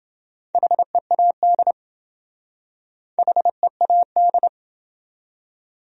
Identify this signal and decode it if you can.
Morse code, heard as a keyed tone, about 27 wpm, 700 Hz. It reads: HEAD HEAD